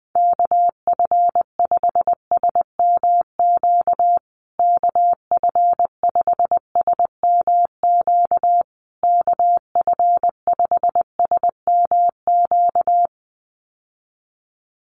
Morse code, heard as a keyed tone, about 20 wpm, 700 Hz. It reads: KF5SMQ KF5SMQ KF5SMQ
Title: USA Call Signs Repeated 3x 20wpm